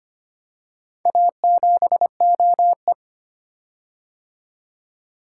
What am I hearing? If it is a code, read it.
Morse code, heard as a keyed tone, about 25 wpm, 700 Hz. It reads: A7OE
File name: practice / 25wpm / Four Letter-Number Combinations - 25wpm